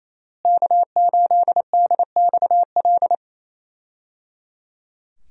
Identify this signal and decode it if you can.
Morse code, heard as a keyed tone, about 28 wpm, 700 Hz. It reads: K8DXL
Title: USA Call Signs - 5 Characters - Repeated 3x - 28wpm